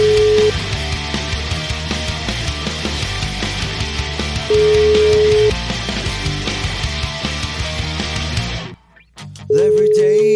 Ringback tone (Europe, 425 Hz), with noise.